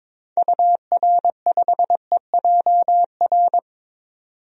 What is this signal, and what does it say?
Morse code, heard as a keyed tone, about 22 wpm, 700 Hz. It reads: UR5EJR
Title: DXCC-100 Call Signs - 6 Characters - 22wpm